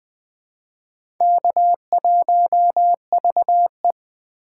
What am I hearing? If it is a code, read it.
Morse code, heard as a keyed tone, about 21 wpm, 700 Hz. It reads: K1VE
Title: USA Call Signs - 4 Characters - 3x Character Spacing - 20wpm